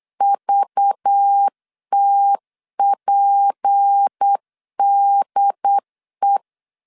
Morse code, heard as a keyed tone, about 8 wpm, 800 Hz. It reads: VTPDE
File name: LEZIONI / 6 PKX0 / 6 PKX 8wpm 3041